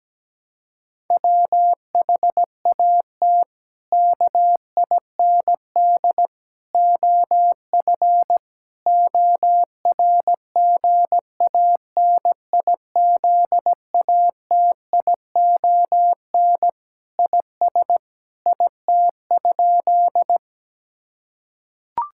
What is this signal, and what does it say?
Morse code, heard as a keyed tone, about 17 wpm, 700 Hz. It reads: WHAT KIND OF ORGANIZATION IS IT?
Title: Sentences from Top 1000 Words - Review Part 5 - 17wpm